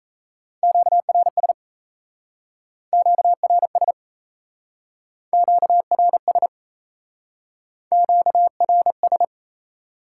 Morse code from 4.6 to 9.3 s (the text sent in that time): QRS QRS